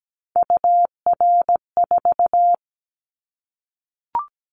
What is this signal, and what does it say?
Morse code, heard as a keyed tone, about 17 wpm, 700 Hz. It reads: UR4
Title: Callsign Prefixes for Top 25 DXCC - 17wpm